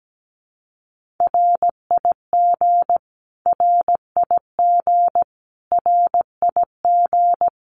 Morse code, heard as a keyed tone, about 17 wpm, 700 Hz. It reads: RIG RIG RIG
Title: Abbreviations Repeated 3x - 17wpm